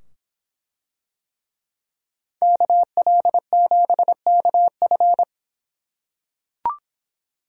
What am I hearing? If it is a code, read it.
Morse code, heard as a keyed tone, about 24 wpm, 700 Hz. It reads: KL7KF